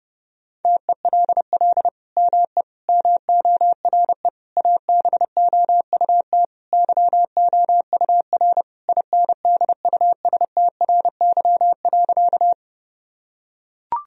Morse code, heard as a keyed tone, about 30 wpm, 700 Hz. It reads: TELL ME MORE ABOUT YOUR INDUSTRY.
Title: Sentences from Top 1000 Words - Review Part 4 - 30wpm